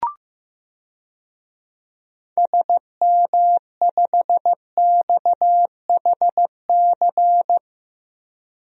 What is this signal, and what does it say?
Morse code, heard as a keyed tone, about 15 wpm, 700 Hz. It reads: SM5XHC
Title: DXCC-100 Call Signs - 6 Characters - 15wpm